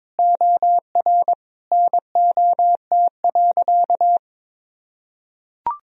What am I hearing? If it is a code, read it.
Morse code, heard as a keyed tone, about 22 wpm, 700 Hz. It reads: OR NOT.
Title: Sentences from Top 1000 Words - Review Part 3 - 22wpm